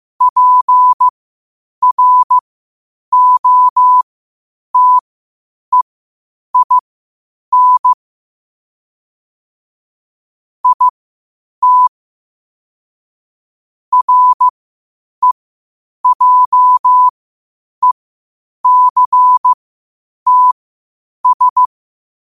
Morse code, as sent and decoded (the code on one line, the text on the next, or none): .--. .-. --- - . .. -. / .. - / .-. .--- . -.-. - ...
PROTEIN IT REJECTS